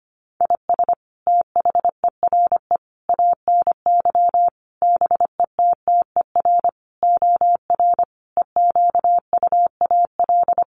Morse code, heard as a keyed tone, about 25 wpm, 700 Hz. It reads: IS THERE ANY BETTER OR EQUAL
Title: Sentences from Top 1400 Words 25wpm